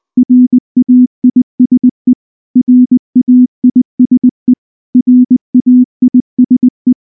Morse code, heard as a keyed tone, about 19 wpm, 250 Hz. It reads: RAISE RAISE RAISE
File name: morse / morse-training-20WPM-3